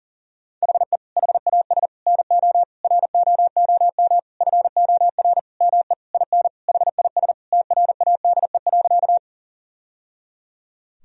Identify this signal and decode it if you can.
Morse code, heard as a keyed tone, about 40 wpm, 690 Hz. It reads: HE HAS NO ROOM FOR ME IN HIS TRADE.
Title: Sentences from Top 1000 Words - Review Part 4 - 40wpm